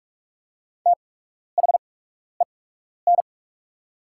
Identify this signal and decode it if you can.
Morse code, heard as a keyed tone, about 46 wpm, spaced as for 12 wpm, 700 Hz. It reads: THEN
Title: 8x Character Spacing - Top 100 Words - 45wpm